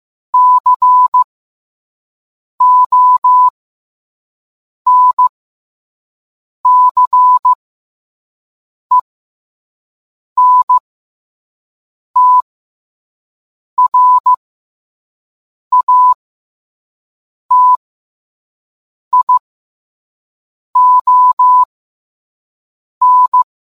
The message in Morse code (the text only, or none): CONCENTRATION